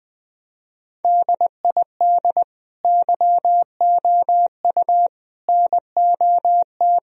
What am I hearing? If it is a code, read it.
Morse code, heard as a keyed tone, about 19 wpm, 700 Hz. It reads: DID YOU NOT
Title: Sentences from Top 1000 Words - Review Part 4 - 20wpm